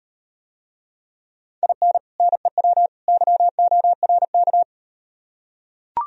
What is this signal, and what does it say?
Morse code, heard as a keyed tone, about 38 wpm, 700 Hz. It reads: IN NEW YORK